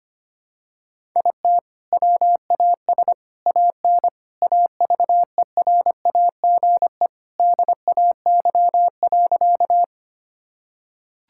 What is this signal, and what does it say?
Morse code, heard as a keyed tone, about 25 wpm, 700 Hz. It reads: IT WAS AN AVERAGE DAY.
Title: Sentences from Top 1000 Words - Review Part 6 - 25wpm